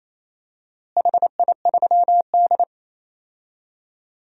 Morse code, heard as a keyed tone, about 28 wpm, 700 Hz. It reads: HI3D
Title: DXCC-100 Call Signs - 4 Characters - Repeated 2x - 28wpm